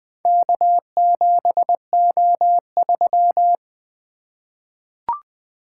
Morse code, heard as a keyed tone, about 20 wpm, 700 Hz. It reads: K7O3
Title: Four Letter-Number Combinations - 20wpm